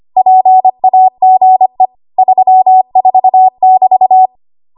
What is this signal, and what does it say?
Morse code, heard as a keyed tone, about 25 wpm, 750 Hz. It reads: PAGE 34<BT>
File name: Archive / 25 WPM / 25WPM